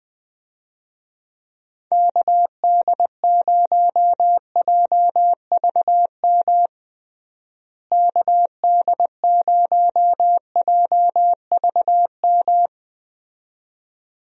Morse code, heard as a keyed tone, about 20 wpm, 700 Hz. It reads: KD0JVM KD0JVM
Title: USA Call Signs - 6 Characters - Repeated 2x - 20wpm